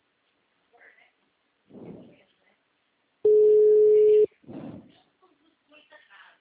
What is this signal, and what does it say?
Ringback tone (Europe, 425 Hz), with speech.